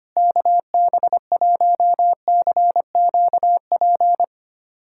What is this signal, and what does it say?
Morse code, heard as a keyed tone, about 25 wpm, 700 Hz. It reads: KB1CQP
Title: USA Callsign DE Callsign 25wpm